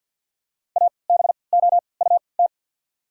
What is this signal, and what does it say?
Morse code, heard as a keyed tone, about 53 wpm, spaced as for 29 wpm, 690 Hz. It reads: ABOUT
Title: 3x Character Spacing - Top 100 Words - 50wpm